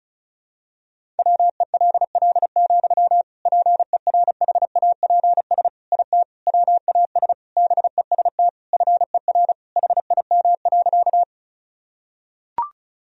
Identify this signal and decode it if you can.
Morse code, heard as a keyed tone, about 34 wpm, 700 Hz. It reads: WELL, PERHAPS IT WAS BEST FER HIM.
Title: Sentences from Top 500 Words and Abbreviations 35wpm